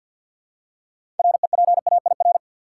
Morse code, heard as a keyed tone, about 53 wpm, 700 Hz. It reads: REPAIR
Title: Six Letter Words - 50wpm